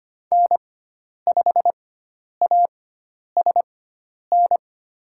Morse code, heard as a keyed tone, about 25 wpm, spaced as for 10 wpm, 700 Hz. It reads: N5ASN
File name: Lesson 16g - Letter c - Call Signs 25wpm